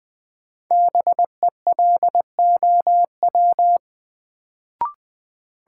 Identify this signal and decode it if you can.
Morse code, heard as a keyed tone, about 21 wpm, 700 Hz. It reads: BELOW
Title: Sentences from Top 1000 Words - Review Part 5 - New Word Drill - 20wpm